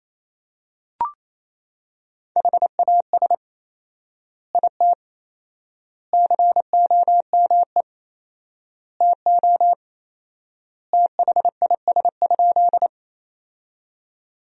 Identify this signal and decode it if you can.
Morse code, heard as a keyed tone, about 28 wpm, 700 Hz. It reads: HAS IT COME TO THIS?
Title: 4x Word-Spacing - Sentences from Top 100 Words - 28wpm